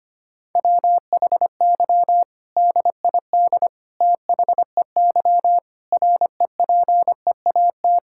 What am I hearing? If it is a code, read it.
Morse code, heard as a keyed tone, about 25 wpm, 700 Hz. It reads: WHY DID THEY REPEAT
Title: Sentences from Top 1600 Words 25wpm